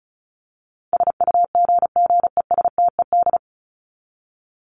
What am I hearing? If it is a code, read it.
Morse code, heard as a keyed tone, about 34 wpm, 700 Hz. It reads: SUGGESTED
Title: ED Suffix - Top 100 Words with Base Words 35wpm